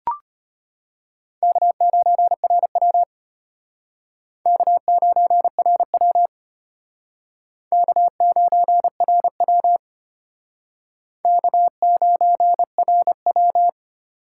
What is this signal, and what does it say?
Morse code, heard as a keyed tone, about 33 wpm, 700 Hz. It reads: K9RW K9RW K9RW K9RW